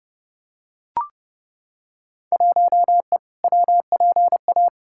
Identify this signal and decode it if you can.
Morse code, heard as a keyed tone, about 30 wpm, 700 Hz. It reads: 1E WPA
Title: Field Day Exchanges - 30wpm